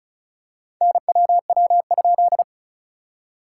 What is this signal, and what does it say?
Morse code, heard as a keyed tone, about 34 wpm, 660 Hz. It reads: NWW?
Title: SOTA QSO Practice 35wpm